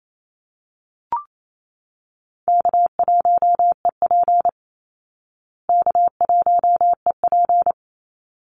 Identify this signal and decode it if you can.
Morse code, heard as a keyed tone, about 28 wpm, 700 Hz. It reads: K1EP K1EP
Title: Easy Call Signs Repeated 2x - 28wpm